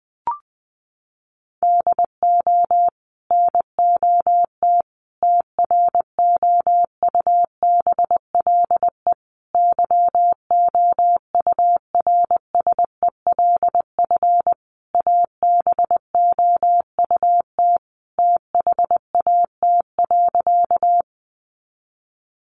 Morse code, heard as a keyed tone, about 19 wpm, 700 Hz. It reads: DO NOT TROUBLE YOURSELF ABOUT THAT.
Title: Sentences from Top 1200 Words 20wpm